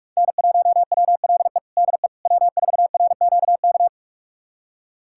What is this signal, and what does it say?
Morse code, heard as a keyed tone, about 47 wpm, 700 Hz. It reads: N1WLE DE W4RQK
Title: USA Callsign DE Callsign - 5 Characters - 45wpm